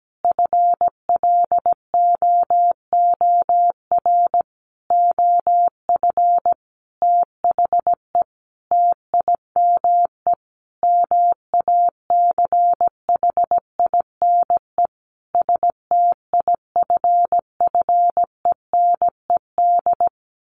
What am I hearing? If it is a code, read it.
Morse code, heard as a keyed tone, about 17 wpm, 700 Hz. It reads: FLOOR OF THE TIME MACHINE STIFFENED
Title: A Sound Of Thunder 17wpm